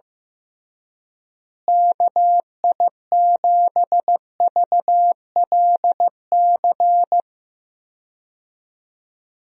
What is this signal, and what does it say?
Morse code, heard as a keyed tone, about 15 wpm, 700 Hz. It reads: KI7VLC
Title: USA Call Signs - 6 Characters - Repeated 3x - 15wpm